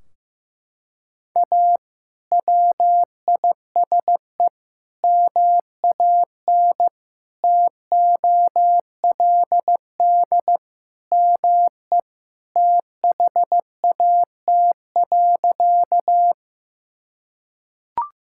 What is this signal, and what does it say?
Morse code, heard as a keyed tone, about 15 wpm, 700 Hz. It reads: A WISE MAN TOLD ME THAT.
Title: Sentences from Top 1600 Words 15wpm